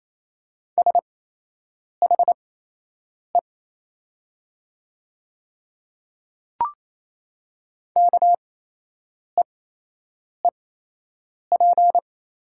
Morse code, heard as a keyed tone, about 28 wpm, spaced as for 8 wpm, 700 Hz. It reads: SHE KEEP
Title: 8x Character Spacing - Top 300 Words - 28wpm